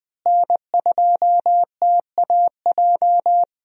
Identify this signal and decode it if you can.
Morse code, heard as a keyed tone, about 21 wpm, 700 Hz. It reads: N2TAJ
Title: USA Call Signs - 5 Characters - Repeated 2x - 20wpm